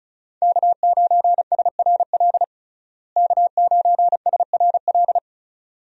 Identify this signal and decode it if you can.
Morse code, heard as a keyed tone, about 36 wpm, 700 Hz. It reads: K9SRL K9SRL
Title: USA Call Signs - 5 Characters - Repeated 2x - 35wpm